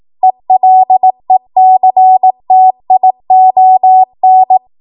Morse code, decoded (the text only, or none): ELECTION